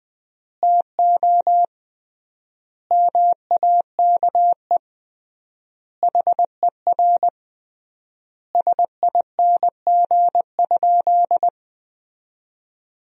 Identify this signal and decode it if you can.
Morse code, heard as a keyed tone, about 20 wpm, 700 Hz. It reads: TO MAKE HER SING?
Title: 3x Word-Spacing - Sentences from Top 400 Words - 20wpm